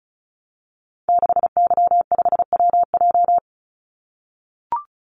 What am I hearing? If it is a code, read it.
Morse code, heard as a keyed tone, about 34 wpm, 700 Hz. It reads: 6Y5WJ
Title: Call Signs - 35wpm